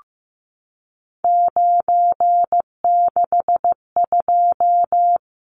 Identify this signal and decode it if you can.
Morse code, heard as a keyed tone, about 15 wpm, 700 Hz. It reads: 962
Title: Numbers - 3-Digit 15wpm